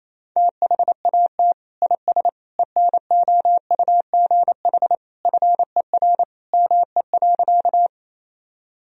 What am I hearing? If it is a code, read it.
Morse code, heard as a keyed tone, about 27 wpm, 700 Hz. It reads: THAT IS ENOUGH FER ME.
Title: Sentences from Top 400 Words and Abbreviations 28wpm